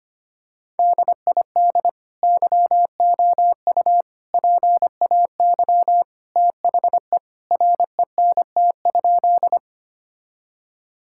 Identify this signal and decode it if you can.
Morse code, heard as a keyed tone, about 25 wpm, 700 Hz. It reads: DID YOU PAY THE RENT?
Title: Sentences from Top 1000 Words - Review Part 5 - 25wpm